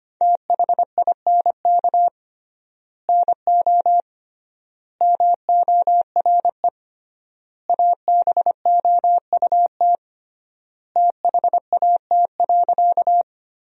Morse code, heard as a keyed tone, about 25 wpm, 700 Hz. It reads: THINK NO MORE ABOUT THAT.
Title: 3x Word-Spacing - Sentences from Top 200 Words - 25wpm